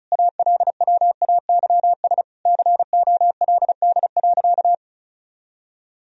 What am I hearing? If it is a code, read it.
Morse code, heard as a keyed tone, about 34 wpm, 700 Hz. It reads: ALWAYS COLD.
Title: Sentences from Top 1000 Words - Review Part 1 - 35wpm